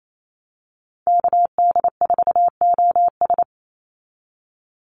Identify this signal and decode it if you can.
Morse code, heard as a keyed tone, about 28 wpm, 700 Hz. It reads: KD4OS
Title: Call Signs Repeated 3x - 28wpm